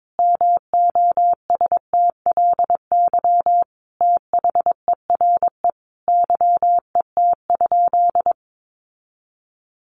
Morse code, heard as a keyed tone, about 22 wpm, 700 Hz. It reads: MOSTLY THERE YET?
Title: Sentences from Top 1300 Words 22wpm